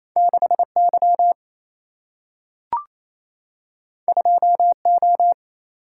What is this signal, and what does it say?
Morse code, heard as a keyed tone, about 28 wpm, 700 Hz. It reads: 6 Y 2 O
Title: Single-Letter Single-Number 28wpm